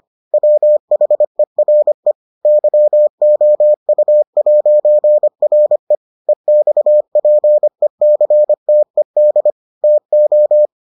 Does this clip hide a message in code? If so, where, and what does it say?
Yes: Morse code from 0.3 to 10.7 s, heard as a keyed tone, about 25 wpm, 600 Hz. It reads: WHERE YOU'RE EXPECTED TO